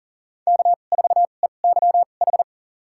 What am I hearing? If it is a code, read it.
Morse code, heard as a keyed tone, about 40 wpm, spaced as for 29 wpm, 700 Hz. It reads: K4EYH
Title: USA Call Signs - 2x Character Spacing - 40wpm